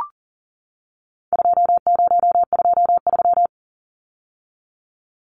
Morse code, heard as a keyed tone, about 40 wpm, 700 Hz. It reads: 2023
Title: Numbers - 4-Digit 40wpm